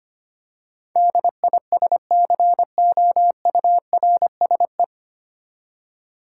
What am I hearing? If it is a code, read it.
Morse code, heard as a keyed tone, about 25 wpm, 700 Hz. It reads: DISCOURSE
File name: Nine Letter Words - 25wpm